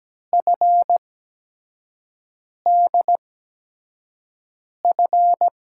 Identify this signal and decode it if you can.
Morse code, heard as a keyed tone, about 17 wpm, spaced as for 5 wpm, 700 Hz. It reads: FDF
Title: USA Call Signs - 6 Characters - 8x Character Spacing - 17wpm